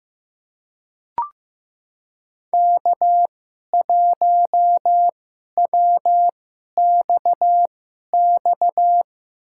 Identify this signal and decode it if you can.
Morse code, heard as a keyed tone, about 15 wpm, spaced as for 11 wpm, 700 Hz. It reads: K1WXX
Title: USA Call Signs - 5 Characters - 2x Character Spacing - 15wpm